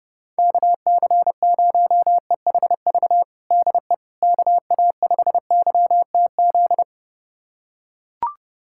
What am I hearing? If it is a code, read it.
Morse code, heard as a keyed tone, about 30 wpm, 700 Hz. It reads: KC0EHV DE KA5YTZ
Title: USA Callsign DE Callsign - 6 Characters - 30wpm